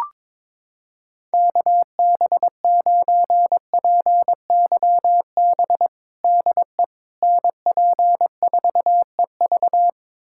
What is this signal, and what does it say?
Morse code, heard as a keyed tone, about 22 wpm, 700 Hz. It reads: KB9PYB DE NP4EV
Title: USA Callsign DE Callsign 22wpm